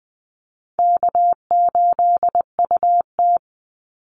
Morse code, heard as a keyed tone, about 20 wpm, 700 Hz. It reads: K8UT
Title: Easy Call Signs Repeated 2x - 20wpm